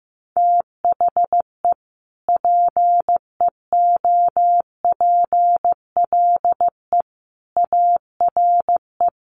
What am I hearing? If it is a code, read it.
Morse code, heard as a keyed tone, about 15 wpm, 700 Hz. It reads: THE PEOPLE ARE
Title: Sentences from Top 1300 Words 15wpm